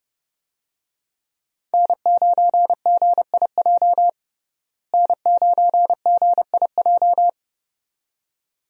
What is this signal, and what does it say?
Morse code, heard as a keyed tone, about 30 wpm, 700 Hz. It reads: N9GIJ N9GIJ
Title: USA Call Signs - 5 Characters - Repeated 2x - 30wpm